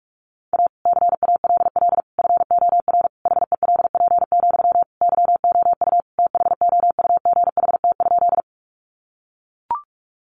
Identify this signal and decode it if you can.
Morse code, heard as a keyed tone, about 44 wpm, 700 Hz. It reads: A CALL FOR HELP, YOU THOUGHT?
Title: Sentences from Top 300 Words 45wpm